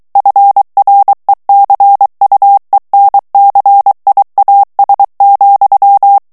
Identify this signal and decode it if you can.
Morse code, heard as a keyed tone, about 23 wpm, 800 Hz. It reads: FRECUENCIAS,
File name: morse mp3 / ESPAGNOL TEXTE EN CLAIR 2 VITESSE 22 MOTS MINUTES